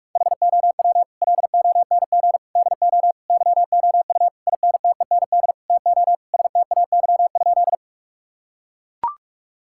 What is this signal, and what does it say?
Morse code, heard as a keyed tone, about 41 wpm, 700 Hz. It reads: HOW LONG DO YOU INTEND TO STAY?
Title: Sentences from Top 1600 Words 45wpm